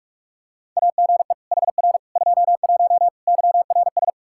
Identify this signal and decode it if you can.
Morse code, heard as a keyed tone, about 41 wpm, 700 Hz. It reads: AGE HR 21 YRS
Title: Practice QSO - Exchange 3 - More Info - 45wpm